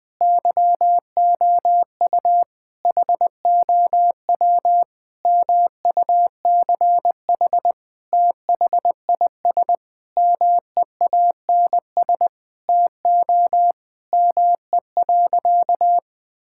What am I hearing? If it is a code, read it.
Morse code, heard as a keyed tone, about 20 wpm, 700 Hz. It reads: YOU HOW MUCH THIS MEANS TO ME.
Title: Sentences with 5-Word Phrases - 20wpm